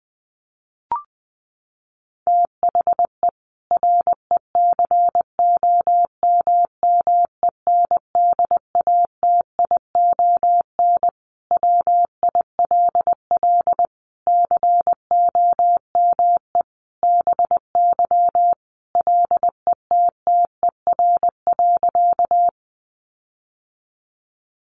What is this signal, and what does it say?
Morse code, heard as a keyed tone, about 20 wpm, 700 Hz. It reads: THE RECOMMENDATION WILL COME BY LETTER.
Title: Sentences from Top 1400 Words 20wpm